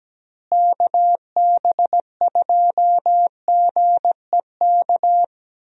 Morse code, heard as a keyed tone, about 17 wpm, 700 Hz. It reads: KB2GEK
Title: USA Call Signs - 6 Characters - Repeated 2x - 17wpm